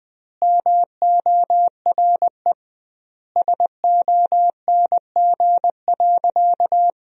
Morse code, heard as a keyed tone, about 20 wpm, 700 Hz. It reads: MORE SONG.
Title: 2x Word-Spacing - Sentences from Top 400 Words - 20wpm